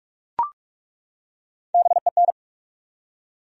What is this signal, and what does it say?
Morse code, heard as a keyed tone, about 42 wpm, 700 Hz. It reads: BEN
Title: USA Common Names - Max 5 Letters - 45wpm